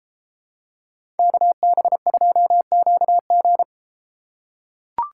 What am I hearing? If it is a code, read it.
Morse code, heard as a keyed tone, about 33 wpm, 700 Hz. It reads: KB2QG